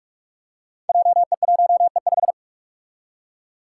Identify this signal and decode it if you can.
Morse code, heard as a keyed tone, about 46 wpm, 700 Hz. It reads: JE1E5
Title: Five Challenging Letter-Number Combinations 45wpm